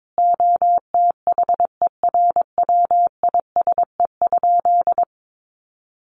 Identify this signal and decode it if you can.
Morse code, heard as a keyed tone, about 22 wpm, 700 Hz. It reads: OTHERWISE?
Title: Sentences from Top 1200 Words 22wpm